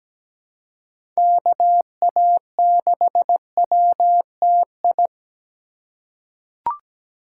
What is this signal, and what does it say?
Morse code, heard as a keyed tone, about 17 wpm, 700 Hz. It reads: KA6WTI
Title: USA Call Signs - 6 Characters - 2x Character Spacing - 17wpm